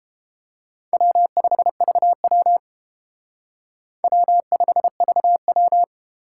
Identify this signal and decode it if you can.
Morse code, heard as a keyed tone, about 30 wpm, 700 Hz. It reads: W5VW W5VW